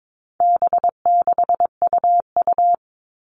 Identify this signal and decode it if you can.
Morse code, heard as a keyed tone, about 22 wpm, 700 Hz. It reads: B6UU
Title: All Cut Numbers - 4-Digit 22wpm